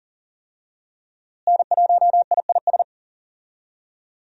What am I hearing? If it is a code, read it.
Morse code, heard as a keyed tone, about 40 wpm, 700 Hz. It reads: N1IIS
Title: USA Call Signs - 5 Characters - Repeated 3x - 40wpm